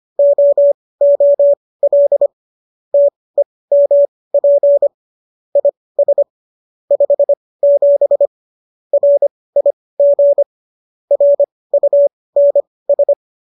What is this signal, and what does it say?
Morse code, heard as a keyed tone, about 25 wpm, spaced as for 18 wpm, 570 Hz. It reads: OOL TEMP IS 57 RIG RUNS
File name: qso201 18